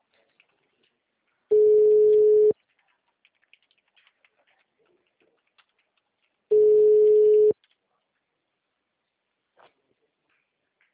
Ringback tone (Europe, 425 Hz).
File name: out- -2004- - - .1632